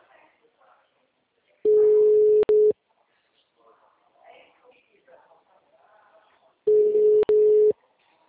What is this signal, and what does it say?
Ringback tone (Europe, 425 Hz), with speech.